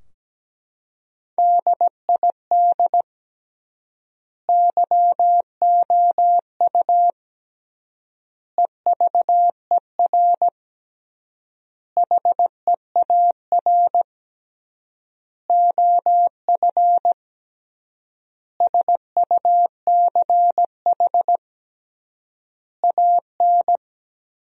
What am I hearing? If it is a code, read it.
Morse code, heard as a keyed tone, about 17 wpm, 700 Hz. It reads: DID YOU EVER HEAR OF SUCH AN
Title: 3x Word-Spacing - Sentences from Top 400 Words - 17wpm